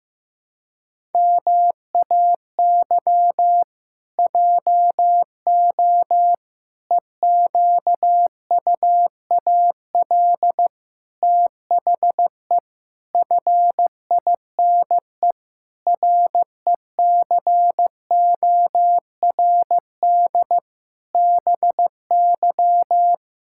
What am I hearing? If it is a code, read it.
Morse code, heard as a keyed tone, about 15 wpm, 700 Hz. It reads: MAY JO EQUAL THE FINE RECORD BY
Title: Pangram Challenge v2 - 15wpm